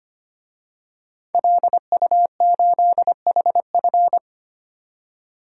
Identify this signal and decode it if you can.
Morse code, heard as a keyed tone, about 25 wpm, 700 Hz. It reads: LU8HF
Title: DXCC-100 Call Signs - 5 Characters - 25wpm